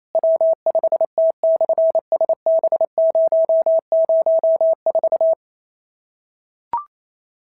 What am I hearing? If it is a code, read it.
Morse code, heard as a keyed tone, about 27 wpm, 650 Hz. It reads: W5T/SB004
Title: SOTA QSO Practice 28wpm